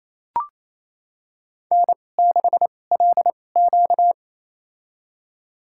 Morse code, heard as a keyed tone, about 28 wpm, spaced as for 20 wpm, 700 Hz. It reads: N6LQ
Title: USA Call Signs - 4 Characters - 2x Character Spacing - 28wpm